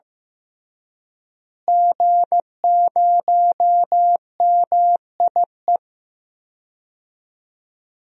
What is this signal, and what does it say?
Morse code, heard as a keyed tone, about 15 wpm, 700 Hz. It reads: G0MIE
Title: DXCC-100 Call Signs - 5 Characters - Repeated 3x - 15wpm